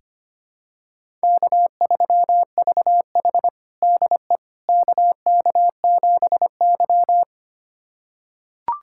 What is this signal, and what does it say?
Morse code, heard as a keyed tone, about 25 wpm, 700 Hz. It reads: K3VH DE KK7Y
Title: USA Callsign DE Callsign - 4 Characters - 25wpm